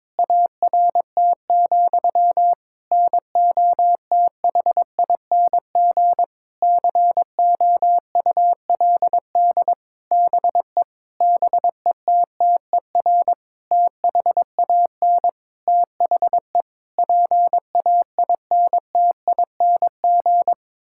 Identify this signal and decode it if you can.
Morse code, heard as a keyed tone, about 22 wpm, 700 Hz. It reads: ART, NOTHING COULD BE BETTER THAN THE PAINTING
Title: Sentences from Top 1000 Words - Review Part 6 - 22wpm